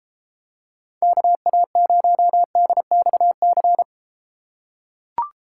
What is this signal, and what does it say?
Morse code, heard as a keyed tone, about 33 wpm, 700 Hz. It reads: KA0DXC